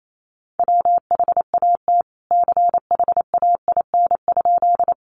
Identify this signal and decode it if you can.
Morse code, heard as a keyed tone, about 28 wpm, 700 Hz. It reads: WHAT CHAIN?
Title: Sentences from Top 1500 Words 28wpm